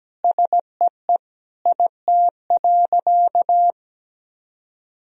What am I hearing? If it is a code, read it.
Morse code, heard as a keyed tone, about 17 wpm, 700 Hz. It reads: SEE IT.
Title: Sentences from Top 1000 Words - Review Part 4 - 17wpm